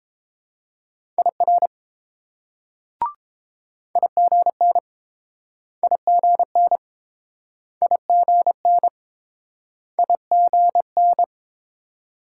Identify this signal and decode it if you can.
Morse code, heard as a keyed tone, about 30 wpm, 700 Hz. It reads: IR IGN IGN IGN IGN